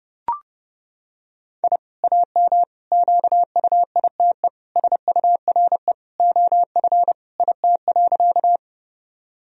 Morse code, heard as a keyed tone, about 30 wpm, 700 Hz. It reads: I AM QUITE SURE OF IT.
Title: Sentences from Top 1000 Words - Review Part 2 - 30wpm